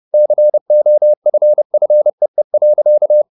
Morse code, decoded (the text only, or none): COFFEE.